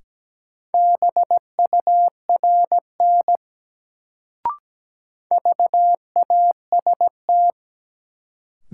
Morse code, heard as a keyed tone, about 17 wpm, 700 Hz. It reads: BURN VAST
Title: Sentences from Top 1300 Words - New Word Drill Alt - 17wpm